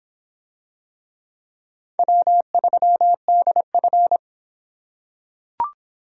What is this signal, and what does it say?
Morse code, heard as a keyed tone, about 25 wpm, 700 Hz. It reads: W3DF